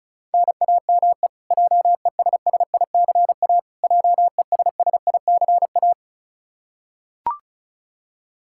Morse code, heard as a keyed tone, about 36 wpm, 700 Hz. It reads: NAME JESSICA JESSICA
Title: Practice QSO - Exchange 1 - Beginning - 35wpm